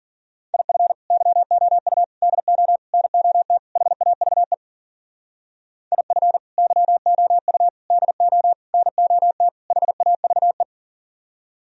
Morse code, heard as a keyed tone, about 52 wpm, 700 Hz. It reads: IF YOU DO NOT HAVE IF YOU DO NOT HAVE